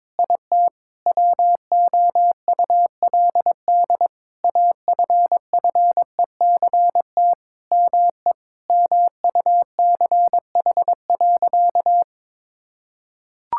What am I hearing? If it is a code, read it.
Morse code, heard as a keyed tone, about 22 wpm, 700 Hz. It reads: IT WOULD AFFECT ME MUCH.
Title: Sentences from Top 1000 Words - Review Part 6 - 22wpm